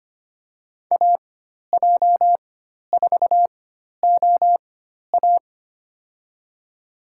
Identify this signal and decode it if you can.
Morse code, heard as a keyed tone, about 25 wpm, spaced as for 12 wpm, 690 Hz. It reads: AJ4OA
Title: USA Call Signs - 4x Character Spacing - 25wpm